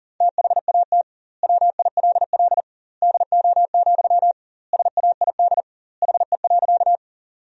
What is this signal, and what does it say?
Morse code, heard as a keyed tone, about 40 wpm, 700 Hz. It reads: THAT WILL DO, SAID HE.
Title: 2x Word-Spacing - Sentences from Top 100 Words - 40wpm